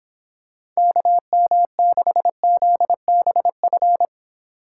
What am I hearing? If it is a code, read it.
Morse code, heard as a keyed tone, about 26 wpm, 700 Hz. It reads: KM6ZBF